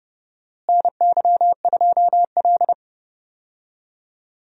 Morse code, heard as a keyed tone, about 31 wpm, 700 Hz. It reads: NY2L
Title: USA Call Signs - 4 Characters - Repeated 3x - 30wpm